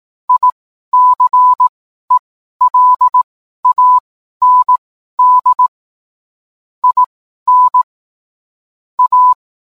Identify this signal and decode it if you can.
Morse code, heard as a keyed tone, about 18 wpm, spaced as for 12 wpm, 1000 Hz. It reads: ICELAND IN A